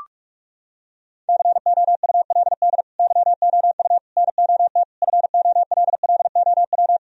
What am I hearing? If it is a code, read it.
Morse code, heard as a keyed tone, about 46 wpm, 700 Hz. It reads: COULD YOU NOT FOLLOW
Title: Sentences from Top 1000 Words - Review Part 1 - 45wpm